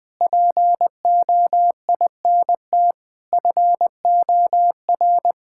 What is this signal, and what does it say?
Morse code, heard as a keyed tone, about 20 wpm, 700 Hz. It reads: POINT FOR
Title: Sentences from Top 1000 Words - Review Part 6 - 20wpm